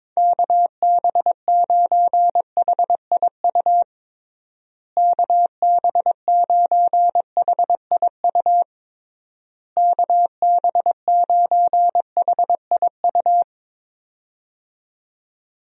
Morse code, heard as a keyed tone, about 22 wpm, 700 Hz. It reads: KB9HIU KB9HIU KB9HIU
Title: USA Call Signs - 6 Characters - Repeated 3x - 22wpm